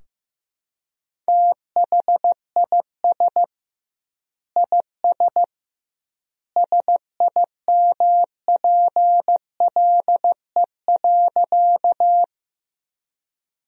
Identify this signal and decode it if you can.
Morse code, heard as a keyed tone, about 15 wpm, 700 Hz. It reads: THIS IS SIMPLE.
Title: 2x Word-Spacing - Sentences from Top 400 Words - 15wpm